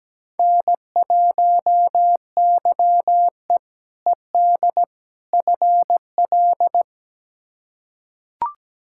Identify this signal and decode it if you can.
Morse code, heard as a keyed tone, about 17 wpm, 700 Hz. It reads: N1YE ED FL
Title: K1USN SST QSO Practice - 17wpm